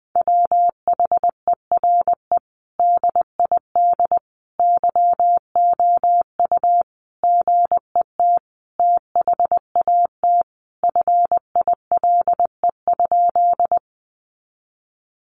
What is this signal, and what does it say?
Morse code, heard as a keyed tone, about 20 wpm, 700 Hz. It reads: WHERE DID YOU GET THAT FILE?
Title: Sentences from Top 1200 Words 20wpm